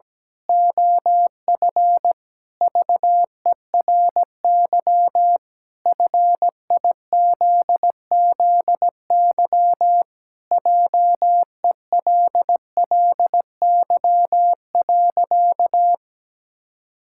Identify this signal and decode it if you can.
Morse code, heard as a keyed tone, about 17 wpm, 700 Hz. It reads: OF VERY FIZZY JELLY.
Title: Pangram Challenge v2 - 17wpm